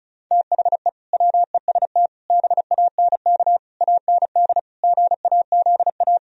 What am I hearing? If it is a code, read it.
Morse code, heard as a keyed tone, about 44 wpm, 700 Hz. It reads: THE WEST BANK AND GAZA